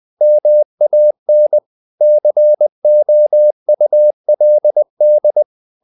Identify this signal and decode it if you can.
Morse code, heard as a keyed tone, about 20 wpm, 600 Hz. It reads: MAN COULD
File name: the old man and the sea part 2 20wpm